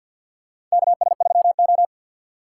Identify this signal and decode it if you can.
Morse code, heard as a keyed tone, about 50 wpm, 700 Hz. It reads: KS3O
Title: USA Call Signs - 4 Characters - 2x Character Spacing - 50wpm